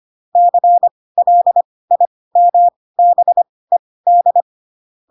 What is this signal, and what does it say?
Morse code, heard as a keyed tone, about 25 wpm, spaced as for 18 wpm, 700 Hz. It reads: CLIMBED
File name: Pinocchio / Pinocchio ch7 12 25 ik1hge